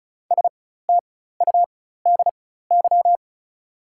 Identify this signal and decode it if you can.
Morse code, heard as a keyed tone, about 36 wpm, spaced as for 16 wpm, 690 Hz. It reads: STUDY
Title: 4x Character Spacing - Top 300 Words - 35wpm